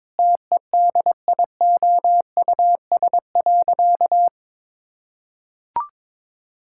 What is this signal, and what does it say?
Morse code, heard as a keyed tone, about 21 wpm, 700 Hz. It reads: TEDIOUS.
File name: Sentences with 7-Word Phrases - 22wpm